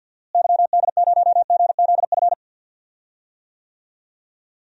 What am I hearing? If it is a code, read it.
Morse code, heard as a keyed tone, about 51 wpm, 700 Hz. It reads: YD0GZF
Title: DXCC-100 Call Signs - 6 Characters - Repeated 3x - 50wpm